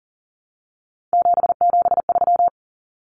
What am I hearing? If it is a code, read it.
Morse code, heard as a keyed tone, about 40 wpm, 700 Hz. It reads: 773
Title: Common Cut Numbers - 3-Digit 40wpm